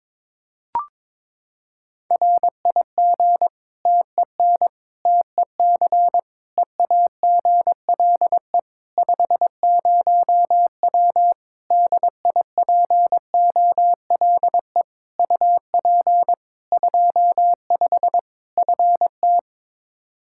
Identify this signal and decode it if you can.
Morse code, heard as a keyed tone, about 22 wpm, 700 Hz. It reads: RIG TEN TEC EAGLE 50W DIPOLE UP 25 FT
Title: Practice QSO - Exchange 2 - Info - 22wpm